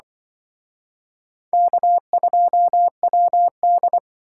Morse code, heard as a keyed tone, about 25 wpm, 700 Hz. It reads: K2WD
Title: Speed Racing - USA 1x2 Call Signs 20wpm